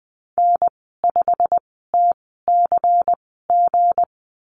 Morse code, heard as a keyed tone, about 19 wpm, spaced as for 14 wpm, 700 Hz. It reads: N5TCG
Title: 186 Lesson 30g - Letter b - Call Signs 20wpm